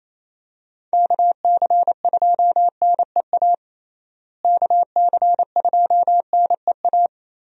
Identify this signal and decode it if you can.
Morse code, heard as a keyed tone, about 28 wpm, 700 Hz. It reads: KC2NEA KC2NEA
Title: USA Call Signs - 6 Characters - Repeated 2x - 28wpm